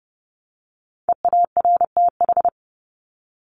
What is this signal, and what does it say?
Morse code, heard as a keyed tone, about 30 wpm, 700 Hz. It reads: EARTH
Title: Top 300 Words Repeated 30wpm